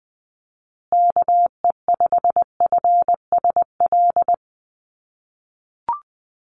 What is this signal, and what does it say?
Morse code, heard as a keyed tone, about 20 wpm, 700 Hz. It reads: KE5FSL
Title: USA Call Signs Repeated 3x 20wpm